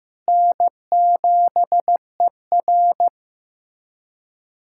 Morse code, heard as a keyed tone, about 15 wpm, 700 Hz. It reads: N7ER
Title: USA Callsign DE Callsign - 4 Characters - 15wpm